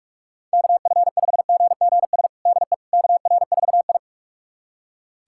Morse code, heard as a keyed tone, about 42 wpm, 700 Hz. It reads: KF5GGS DE KR4I
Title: USA Callsign DE Callsign 45wpm